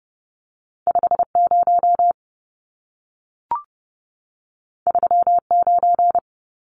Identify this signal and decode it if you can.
Morse code, heard as a keyed tone, about 30 wpm, 700 Hz. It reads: 50 39
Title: Numbers - 2-Digit 30wpm